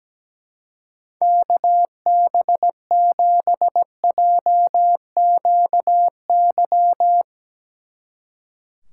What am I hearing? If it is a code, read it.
Morse code, heard as a keyed tone, about 19 wpm, 700 Hz. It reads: KB7JQY
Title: Speed Racing - USA 2x3 Call Signs 17wpm